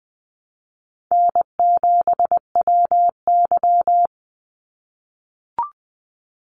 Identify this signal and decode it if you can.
Morse code, heard as a keyed tone, about 20 wpm, 700 Hz. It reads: N7WY
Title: Easy Call Signs - 20wpm